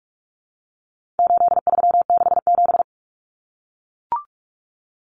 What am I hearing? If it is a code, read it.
Morse code, heard as a keyed tone, about 44 wpm, 700 Hz. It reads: 8367
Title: Common Cut Numbers - 4-Digit 45wpm